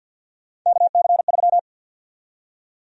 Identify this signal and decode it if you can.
Morse code, heard as a keyed tone, about 47 wpm, 700 Hz. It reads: KC3
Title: USA Call Sign Prefixes - Repeated 2x - 50wpm